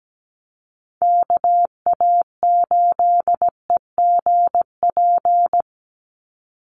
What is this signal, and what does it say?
Morse code, heard as a keyed tone, about 17 wpm, 700 Hz. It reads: KA8EGP
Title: USA Call Signs 17wpm